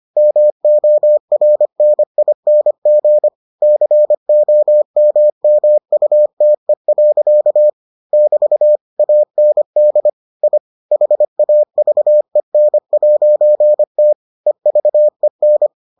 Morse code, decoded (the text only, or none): MORNING COMMUTE. <BT> AND I HAVEN'T EVEN